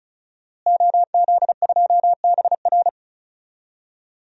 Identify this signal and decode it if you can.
Morse code, heard as a keyed tone, about 36 wpm, 700 Hz. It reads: OZ2BR
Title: DXCC-100 Call Signs - 5 Characters - Repeated 3x - 35wpm